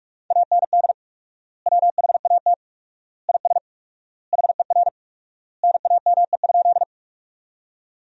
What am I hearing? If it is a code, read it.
Morse code, heard as a keyed tone, about 46 wpm, 700 Hz. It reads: AND WHAT IS HER NAME?
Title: 4x Word-Spacing - Sentences from Top 200 Words - 45wpm